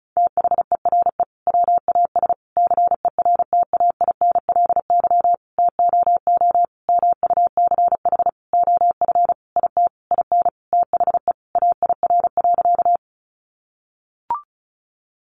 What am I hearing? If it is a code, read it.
Morse code, heard as a keyed tone, about 34 wpm, 700 Hz. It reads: THERE WAS CERTAINLY TOO MUCH OF IT IN THE AIR.
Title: Sentences from Top 900 Words 35wpm